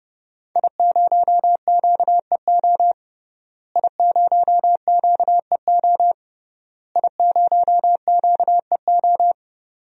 Morse code, heard as a keyed tone, about 30 wpm, 700 Hz. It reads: I0QEO I0QEO I0QEO
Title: DXCC-100 Call Signs - 5 Characters - Repeated 3x - 30wpm